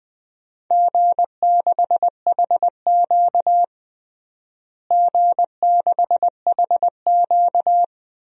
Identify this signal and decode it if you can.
Morse code, heard as a keyed tone, about 20 wpm, 700 Hz. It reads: G6HQ G6HQ
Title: DXCC-100 Call Signs - 4 Characters - Repeated 3x - 20wpm